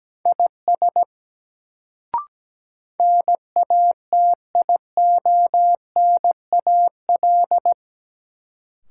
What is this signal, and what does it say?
Morse code, heard as a keyed tone, about 17 wpm, 700 Hz. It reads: IS NATIONAL
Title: Sets of 2 Words - New Word Drill - 17wpm